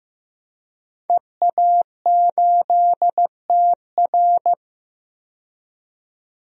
Morse code, heard as a keyed tone, about 15 wpm, 700 Hz. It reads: EA8TR
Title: DXCC-100 Call Signs - 5 Characters - 15wpm